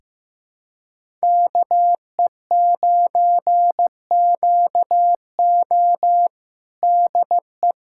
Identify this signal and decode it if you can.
Morse code, heard as a keyed tone, about 15 wpm, 700 Hz. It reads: KE9QO DE
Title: USA Callsign DE Callsign - 5 Characters - 15wpm